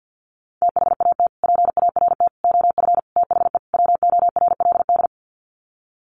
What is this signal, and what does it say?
Morse code, heard as a keyed tone, about 48 wpm, 700 Hz. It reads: THAT PART OF THE WORLD
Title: Sets of 5 Words Encore2 50wpm